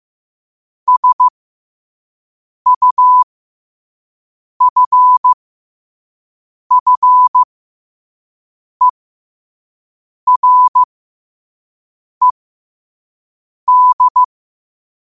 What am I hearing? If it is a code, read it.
Morse code, heard as a keyed tone, about 15 wpm, spaced as for 5 wpm, 1000 Hz. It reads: SUFFERED
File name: qotdfiles / 5wpm